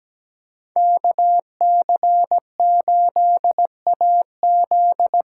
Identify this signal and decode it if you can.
Morse code, heard as a keyed tone, about 17 wpm, 700 Hz. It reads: KC8AZ
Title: USA Call Signs - 6 Characters - 6x Character Spacing - 17wpm